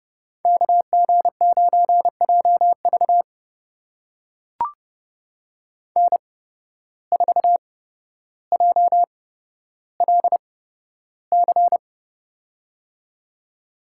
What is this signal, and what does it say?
Morse code, heard as a keyed tone, about 30 wpm, spaced as for 13 wpm, 700 Hz. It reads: KG9JV N4JLC
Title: USA Call Signs - 5 Characters - 8x Character Spacing - 30wpm